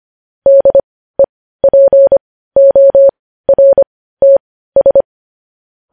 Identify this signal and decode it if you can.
Morse code, heard as a keyed tone, about 25 wpm, spaced as for 15 wpm, 550 Hz. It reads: DEPORTS
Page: news-headlines-in-morse-code-at-15-wpm